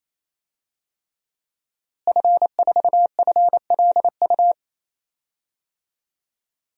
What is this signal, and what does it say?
Morse code, heard as a keyed tone, about 28 wpm, 700 Hz. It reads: F4FLU
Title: DXCC-100 Call Signs - 5 Characters - Repeated 3x - 28wpm